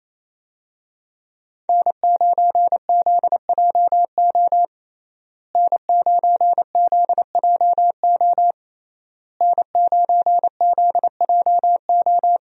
Morse code, heard as a keyed tone, about 27 wpm, 700 Hz. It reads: N9ZJO N9ZJO N9ZJO
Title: USA Call Signs - 5 Characters - Repeated 3x - 28wpm